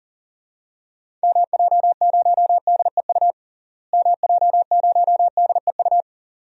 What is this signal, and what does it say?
Morse code, heard as a keyed tone, about 40 wpm, 700 Hz. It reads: MJ0DEU MJ0DEU
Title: DXCC-100 Call Signs - 6 Characters - Repeated 2x - 40wpm